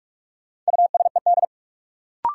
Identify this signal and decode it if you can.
Morse code, heard as a keyed tone, about 46 wpm, 700 Hz. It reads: USED
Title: Sentences from Top 600 Words - New Word Drill - 45wpm